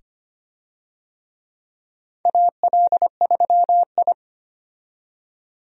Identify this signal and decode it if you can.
Morse code, heard as a keyed tone, about 25 wpm, 700 Hz. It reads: AL3I
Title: USA Call Signs - 4 Characters - 25wpm